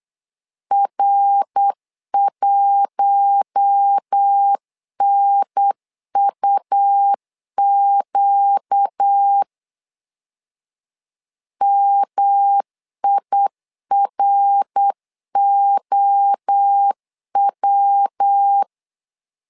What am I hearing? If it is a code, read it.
Morse code, heard as a keyed tone, about 8 wpm, 800 Hz. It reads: R1NUQ MIROW
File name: LEZIONI / 4 Q1 8wpm 3035